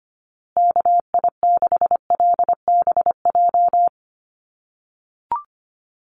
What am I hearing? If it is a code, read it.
Morse code, heard as a keyed tone, about 25 wpm, 700 Hz. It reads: KI6LBJ
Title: USA Call Signs Repeated 2x 25wpm